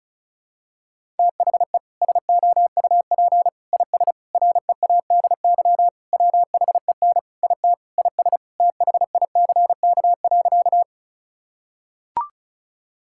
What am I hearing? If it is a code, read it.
Morse code, heard as a keyed tone, about 36 wpm, 700 Hz. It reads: THE SOUP IS READY WHEN IT IS THICK.
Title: Sentences from Top 1600 Words 35wpm